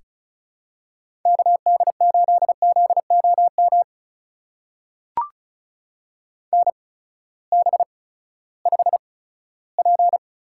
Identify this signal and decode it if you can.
Morse code, heard as a keyed tone, about 36 wpm, spaced as for 18 wpm, 700 Hz. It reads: KD8ZOM NB5P
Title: USA Call Signs - 8x Character Spacing - 35wpm